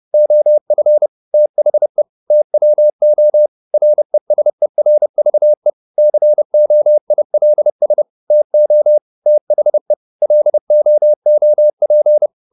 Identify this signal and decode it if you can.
Morse code, heard as a keyed tone, about 30 wpm, 600 Hz. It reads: OF THE TWO RESERVE COILS TO THE LOOP
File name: the old man and the sea / the old man and the sea 1 30wpm